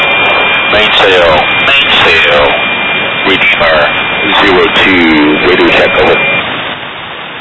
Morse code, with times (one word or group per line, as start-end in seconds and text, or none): none